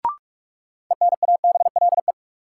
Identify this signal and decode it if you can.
Morse code, heard as a keyed tone, about 44 wpm, 700 Hz. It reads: ENABLE
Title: Sentences from Top 1500 Words - New Word Drill Alt - 45wpm